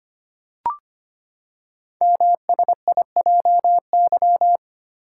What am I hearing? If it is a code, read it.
Morse code, heard as a keyed tone, about 25 wpm, 700 Hz. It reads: MSIJY
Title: Code Groups - 5 Letters - 25wpm